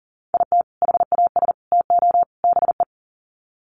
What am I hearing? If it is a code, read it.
Morse code, heard as a keyed tone, about 40 wpm, 700 Hz. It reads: IT HAS TO BE
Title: Sets of 4 Words 40wpm